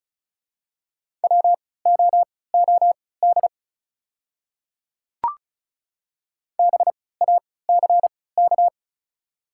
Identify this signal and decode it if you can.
Morse code, heard as a keyed tone, about 36 wpm, spaced as for 20 wpm, 700 Hz. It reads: WOOD BACK
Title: 3x Character Spacing - Top 300 Words - 35wpm